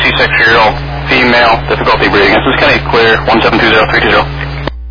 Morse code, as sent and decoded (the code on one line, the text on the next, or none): none